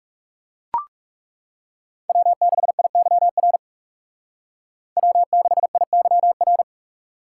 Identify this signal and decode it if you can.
Morse code, heard as a keyed tone, about 39 wpm, 700 Hz. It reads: W6IYR W6IYR